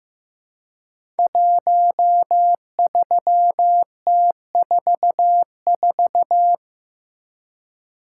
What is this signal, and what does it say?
Morse code, heard as a keyed tone, about 15 wpm, 700 Hz. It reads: 13T44
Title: Code Groups - 5 Letter:Numbers - 15wpm